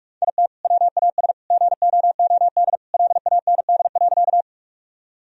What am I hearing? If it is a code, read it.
Morse code, heard as a keyed tone, about 46 wpm, 700 Hz. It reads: IT WAS GOOD LAND.
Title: Sentences from Top 1000 Words - Review Part 1 - 45wpm